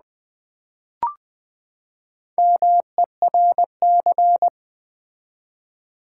Morse code, heard as a keyed tone, about 20 wpm, 700 Hz. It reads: MERC
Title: Build A Word - 4 Letters At A Time - 8-Letter Words - 20wpm